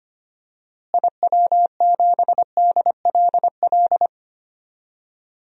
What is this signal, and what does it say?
Morse code, heard as a keyed tone, about 25 wpm, 700 Hz. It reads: IW7DLL
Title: DXCC-100 Call Signs - 6 Characters - Repeated 3x - 25wpm